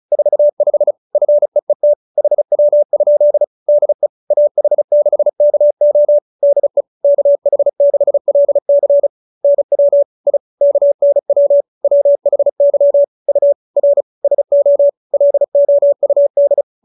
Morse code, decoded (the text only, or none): 45 FEET HW? DE AH6KO DE KH6LC NW I KNW WHY U R SO LOUD